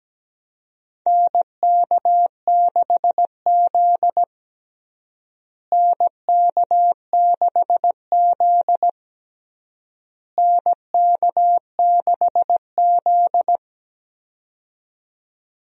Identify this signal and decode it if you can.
Morse code, heard as a keyed tone, about 17 wpm, 700 Hz. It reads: NK6Z NK6Z NK6Z
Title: USA Call Signs - 4 Characters - Repeated 3x - 17wpm